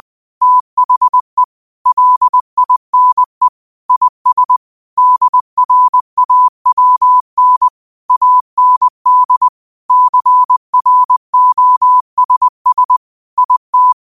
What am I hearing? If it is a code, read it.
Morse code, heard as a keyed tone, about 20 wpm, 1000 Hz. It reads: THE LINE IS DRAWN AND CROSS IT